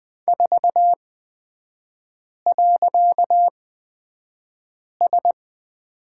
Morse code, heard as a keyed tone, about 20 wpm, spaced as for 5 wpm, 700 Hz. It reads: 4.S
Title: 255 Lesson 41e - Period - Three Characters 20wpm